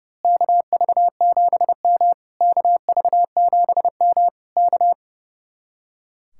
Morse code, heard as a keyed tone, about 29 wpm, 700 Hz. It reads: KV7M KV7M K
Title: Speed-Racing Common QSO Phrases Encore - 30wpm